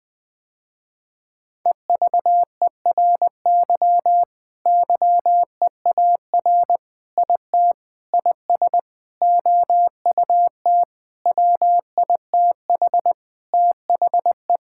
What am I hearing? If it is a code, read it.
Morse code, heard as a keyed tone, about 20 wpm, 700 Hz. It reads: EVERY YEAR IT IS OUT WITH THE
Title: Sentences with 9-Word Phrases - Spoken Beginning - 20wpm